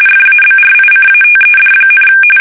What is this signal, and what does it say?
RTTY, about 45.45 baud. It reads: FABRICATED,
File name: fsk850 / usnavy3